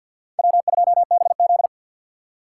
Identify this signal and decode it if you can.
Morse code, heard as a keyed tone, about 61 wpm, 700 Hz. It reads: W2BZ
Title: Speed Racing - USA 1x2 Call Signs 50wpm